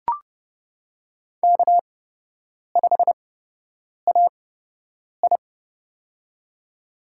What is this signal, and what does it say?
Morse code, heard as a keyed tone, about 30 wpm, spaced as for 8 wpm, 700 Hz. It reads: K5AI
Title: USA Call Signs - 4 Characters - 8x Character Spacing - 30wpm